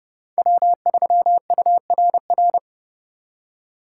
Morse code, heard as a keyed tone, about 37 wpm, 700 Hz. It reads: W3URR
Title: Speed Racing - USA Call Signs 30wpm